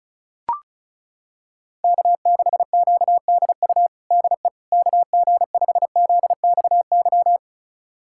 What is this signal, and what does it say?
Morse code, heard as a keyed tone, about 34 wpm, 690 Hz. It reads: K6QDU DE KG5ZXY
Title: USA Callsign DE Callsign 35wpm